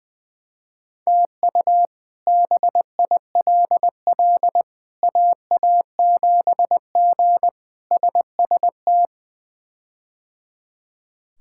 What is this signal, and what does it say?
Morse code, heard as a keyed tone, about 21 wpm, 700 Hz. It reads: TU BILL AA7G SST
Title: K1USN SST QSO Practice - 20wpm